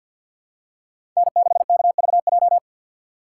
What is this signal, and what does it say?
Morse code, heard as a keyed tone, about 49 wpm, 700 Hz. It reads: N6KVJ
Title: USA Call Signs - 5 Characters - 3x Character Spacing - 50wpm